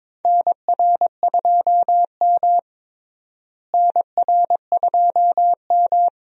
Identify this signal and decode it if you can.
Morse code, heard as a keyed tone, about 22 wpm, 700 Hz. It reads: NR2M NR2M
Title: USA Call Signs - 4 Characters - Repeated 3x - 22wpm